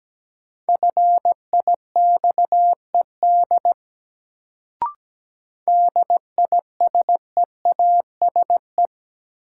Morse code, heard as a keyed tone, about 17 wpm, 700 Hz. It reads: FIXED DISEASE
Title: Sentences from Top 1100 Words - New Word Drill Alt - 17wpm